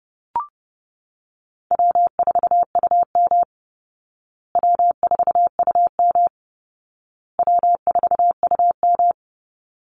Morse code, heard as a keyed tone, about 30 wpm, 700 Hz. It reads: W4UM W4UM W4UM
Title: Easy Call Signs Repeated 3x - 30wpm